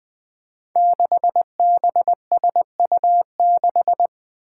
Morse code, heard as a keyed tone, about 20 wpm, 700 Hz. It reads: 6BSU6
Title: Five Challenging Letter-Number Combinations 20wpm